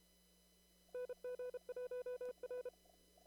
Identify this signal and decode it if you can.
Morse code, heard as a keyed tone, about 32 wpm, 500 Hz. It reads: NG1R
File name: CW / kt5j wpx cw-op-k5pi